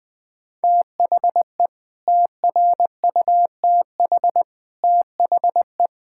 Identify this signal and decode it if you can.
Morse code, heard as a keyed tone, about 21 wpm, 700 Hz. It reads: THE TRUTH THE
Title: Sets of 9 Words - 20wpm